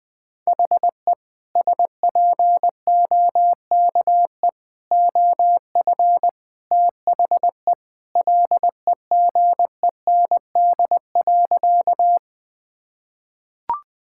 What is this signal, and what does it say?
Morse code, heard as a keyed tone, about 21 wpm, 700 Hz. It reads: HE SPOKE OF THE LEGEND.
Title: Sentences with 2-Word Phrases - 20wpm